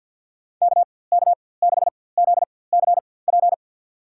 Morse code, heard as a keyed tone, about 52 wpm, spaced as for 23 wpm, 700 Hz. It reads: KK6ZCP
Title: USA Call Signs - 6 Characters - 4x Character Spacing - 50wpm